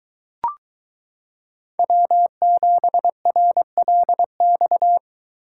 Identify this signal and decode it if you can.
Morse code, heard as a keyed tone, about 23 wpm, 700 Hz. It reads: W7RLX